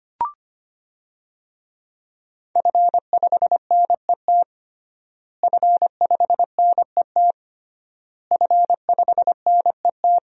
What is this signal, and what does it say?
Morse code, heard as a keyed tone, about 25 wpm, 700 Hz. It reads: F5NET F5NET F5NET
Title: DXCC-100 Call Signs - 5 Characters - Repeated 3x - 25wpm